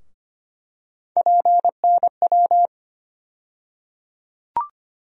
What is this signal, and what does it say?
Morse code, heard as a keyed tone, about 25 wpm, 700 Hz. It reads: PNW
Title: USA Call Sign Postfixes - 3 Letters 25wpm